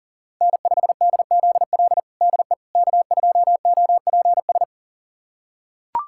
Morse code, heard as a keyed tone, about 40 wpm, 700 Hz. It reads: N5DZL DE K2OPS
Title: USA Callsign DE Callsign - 5 Characters - 40wpm